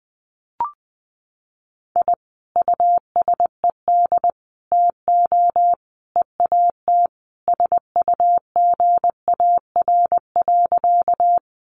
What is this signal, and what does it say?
Morse code, heard as a keyed tone, about 20 wpm, 700 Hz. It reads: I USED TO EAT SUGAR.
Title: Sentences from Top 1200 Words 20wpm